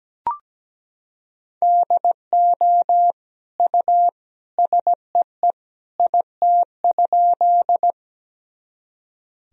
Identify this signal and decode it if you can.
Morse code, heard as a keyed tone, about 17 wpm, 700 Hz. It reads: DO U SEE IT?
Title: Sentences from Top 100 Words and Abbreviations 17wpm